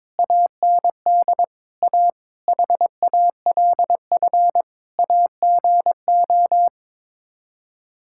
Morse code, heard as a keyed tone, about 21 wpm, 700 Hz. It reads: AND A HALF AGO
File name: practice / 22wpm / Sets of 7 Words - 22wpm